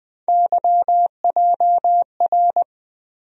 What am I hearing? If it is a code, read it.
Morse code, heard as a keyed tone, about 20 wpm, 700 Hz. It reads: YJR
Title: USA Call Sign Postfixes - 3 Letters 20wpm